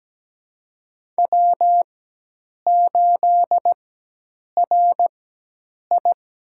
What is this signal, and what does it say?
Morse code, heard as a keyed tone, about 17 wpm, spaced as for 8 wpm, 700 Hz. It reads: W8RI
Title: USA Call Signs - 4x Character Spacing - 17wpm